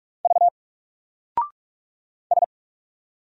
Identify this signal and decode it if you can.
Morse code, heard as a keyed tone, about 44 wpm, 700 Hz. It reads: V S
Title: Single Character (Letter:Number:Punctuation:Prosign) 45wpm